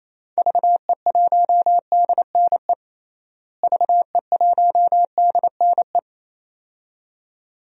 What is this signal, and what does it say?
Morse code, heard as a keyed tone, about 28 wpm, 700 Hz. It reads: VE1DNE VE1DNE
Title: DXCC-100 Call Signs - 6 Characters - Repeated 2x - 28wpm